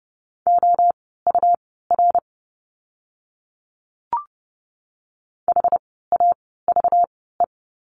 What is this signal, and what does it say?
Morse code, heard as a keyed tone, about 30 wpm, spaced as for 17 wpm, 700 Hz. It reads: OUR HAVE
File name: course / 30:10wpm / 237 Lesson 38f - Letter q - Words 30wpm